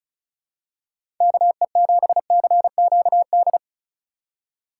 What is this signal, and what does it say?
Morse code, heard as a keyed tone, about 34 wpm, 700 Hz. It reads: KE7CQD
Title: USA Call Signs - 6 Characters - 3x Character Spacing - 35wpm